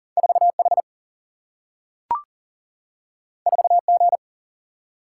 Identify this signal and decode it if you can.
Morse code, heard as a keyed tone, about 39 wpm, 700 Hz. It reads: 4H 4G
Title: Single-Letter Single-Number 40wpm